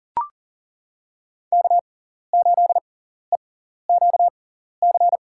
Morse code, heard as a keyed tone, about 40 wpm, spaced as for 14 wpm, 700 Hz. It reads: K8EQC
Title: USA Call Signs - 5 Characters - 6x Character Spacing - 40wpm